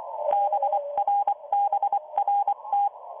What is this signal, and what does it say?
Morse code, heard as a keyed tone, about 24 wpm, 810 Hz, with radio noise. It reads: BRBRT